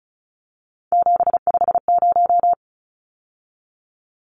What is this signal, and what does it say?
Morse code, heard as a keyed tone, about 34 wpm, 700 Hz. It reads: 750
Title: Numbers - 3-Digit 35wpm